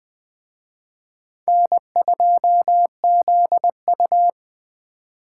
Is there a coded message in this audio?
Yes: Morse code from 1.5 to 4.3 s, heard as a keyed tone, about 21 wpm, 700 Hz. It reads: N2ZU